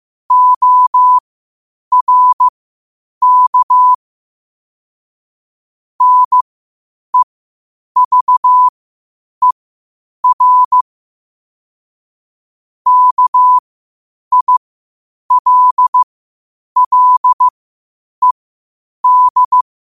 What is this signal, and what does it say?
Morse code, heard as a keyed tone, about 15 wpm, spaced as for 8 wpm, 1000 Hz. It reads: ORK NEVER KILLED